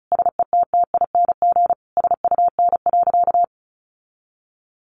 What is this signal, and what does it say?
Morse code, heard as a keyed tone, about 34 wpm, 700 Hz. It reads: SETTING SUN.
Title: Sentences from Top 1300 Words 35wpm